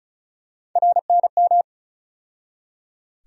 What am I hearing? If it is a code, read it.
Morse code, heard as a keyed tone, about 34 wpm, 700 Hz. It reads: RNM
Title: USA Call Sign Postfixes - 3 Letters 35wpm